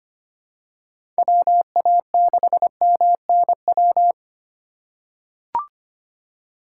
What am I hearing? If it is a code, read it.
Morse code, heard as a keyed tone, about 25 wpm, 700 Hz. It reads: WA6MNW
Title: USA Call Signs - 6 Characters - 4x Character Spacing - 25wpm